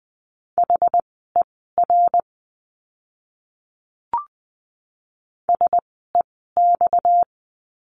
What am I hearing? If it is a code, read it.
Morse code, heard as a keyed tone, about 20 wpm, spaced as for 14 wpm, 700 Hz. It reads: HER SEX
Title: 230 Lesson 37f - Letter x - Words 20wpm